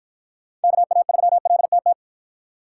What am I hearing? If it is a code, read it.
Morse code, heard as a keyed tone, about 51 wpm, 700 Hz. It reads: KA3LTT